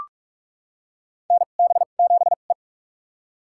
Morse code, heard as a keyed tone, about 41 wpm, spaced as for 31 wpm, 700 Hz. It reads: NB7E
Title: USA Call Signs - 4 Characters - 2x Character Spacing - 45wpm